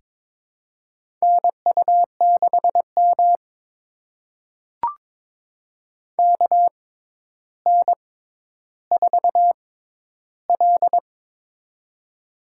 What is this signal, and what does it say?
Morse code, heard as a keyed tone, about 22 wpm, spaced as for 11 wpm, 700 Hz. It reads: NU6M KN4L
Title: USA Call Signs - 4 Characters - 6x Character Spacing - 22wpm